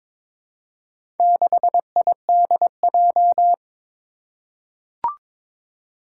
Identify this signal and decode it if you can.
Morse code, heard as a keyed tone, about 22 wpm, 700 Hz. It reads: 6IDJ
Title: Four Challenging Letter-Number Combinations 22wpm